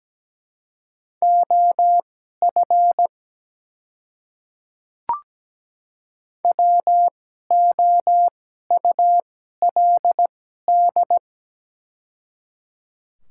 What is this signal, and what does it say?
Morse code, heard as a keyed tone, about 17 wpm, spaced as for 12 wpm, 700 Hz. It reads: OF WOULD
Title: 2x Character Spacing - Top 200 Words - 17wpm